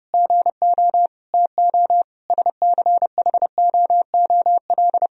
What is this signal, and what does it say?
Morse code, heard as a keyed tone, about 30 wpm, 700 Hz. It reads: GO TO SCHOOL
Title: Sentences from Top 1000 Words - Review Part 3 - 30wpm